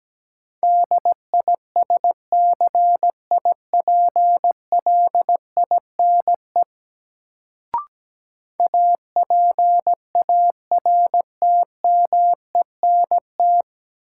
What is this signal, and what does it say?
Morse code, heard as a keyed tone, about 17 wpm, 700 Hz. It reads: DISCIPLINE APARTMENT
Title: Sentences from Top 900 Words - New Word Drill - 17wpm